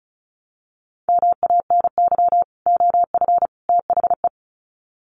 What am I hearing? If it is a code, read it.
Morse code, heard as a keyed tone, about 34 wpm, 700 Hz. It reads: MANY OF THE
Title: Sets of 3 Words 35wpm